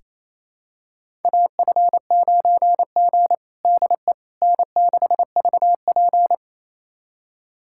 Morse code, heard as a keyed tone, about 28 wpm, 700 Hz. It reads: AF9G DE N6VP
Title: USA Callsign DE Callsign - 4 Characters - 28wpm